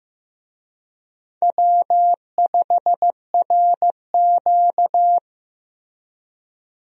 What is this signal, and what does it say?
Morse code, heard as a keyed tone, about 15 wpm, 700 Hz. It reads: W5RQ
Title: USA Call Signs - 4 Characters - Repeated 2x - 15wpm